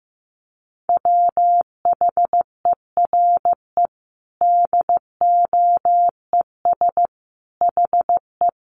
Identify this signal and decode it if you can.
Morse code, heard as a keyed tone, about 15 wpm, 700 Hz. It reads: WHERE DOES HE
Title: Sentences from Top 200 Words 15wpm